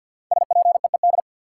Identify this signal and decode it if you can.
Morse code, heard as a keyed tone, about 51 wpm, 700 Hz. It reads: SPEED
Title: Sentences from Top 800 Words - New Word Drill - 50wpm